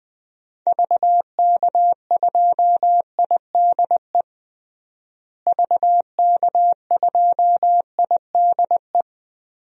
Morse code, heard as a keyed tone, about 20 wpm, 700 Hz. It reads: VK2IDE VK2IDE
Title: DXCC-100 Call Signs - 6 Characters - Repeated 2x - 20wpm